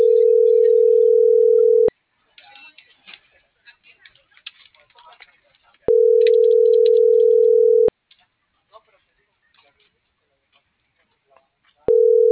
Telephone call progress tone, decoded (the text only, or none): ringback tone